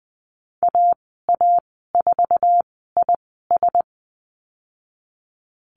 Morse code, heard as a keyed tone, about 20 wpm, spaced as for 15 wpm, 700 Hz. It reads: AA4IS
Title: 055 Lesson 10g - Letter r - Call Signs 20wpm